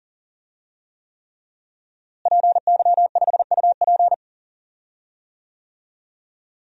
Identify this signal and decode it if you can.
Morse code, heard as a keyed tone, about 40 wpm, 700 Hz. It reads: PY5UP
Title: DXCC-100 Call Signs - 5 Characters - Repeated 3x - 40wpm